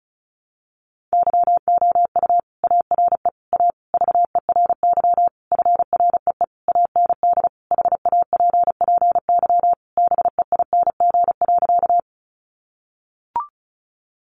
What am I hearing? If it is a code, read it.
Morse code, heard as a keyed tone, about 34 wpm, 700 Hz. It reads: YOU ARE A VERY FREE AND HAPPY BEING.
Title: Sentences from Top 1000 Words 35wpm